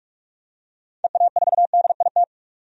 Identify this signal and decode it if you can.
Morse code, heard as a keyed tone, about 46 wpm, 700 Hz. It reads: EA4DIT
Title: DXCC-100 Call Signs - 6 Characters - 45wpm